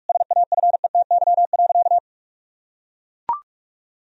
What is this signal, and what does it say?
Morse code, heard as a keyed tone, about 41 wpm, 700 Hz. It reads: SAFETY.
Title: Sentences from Top 1000 Words - Review Part 4 - 45wpm